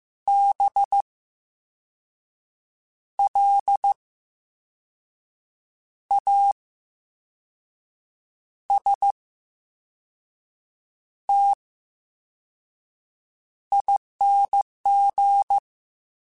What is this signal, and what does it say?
Morse code, heard as a keyed tone, about 15 wpm, spaced as for 5 wpm, 800 Hz. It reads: BLASTING
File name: Special INGs / Special INGs-mp3s / Slow ing @13